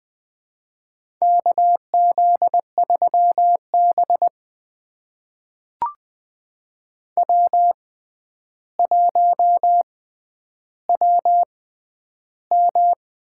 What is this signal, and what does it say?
Morse code, heard as a keyed tone, about 20 wpm, spaced as for 10 wpm, 700 Hz. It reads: KZ3B W1WM
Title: USA Call Signs - 4 Characters - 6x Character Spacing - 20wpm